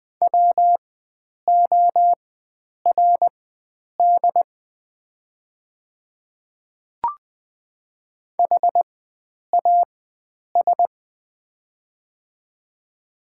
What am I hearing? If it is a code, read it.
Morse code, heard as a keyed tone, about 20 wpm, spaced as for 9 wpm, 700 Hz. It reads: WORD HAS
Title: 4x Character Spacing - Top 100 Words - 20wpm